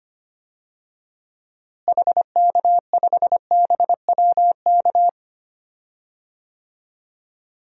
Morse code, heard as a keyed tone, about 25 wpm, 700 Hz. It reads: HK5BWK
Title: DXCC-100 Call Signs - 6 Characters - Repeated 3x - 25wpm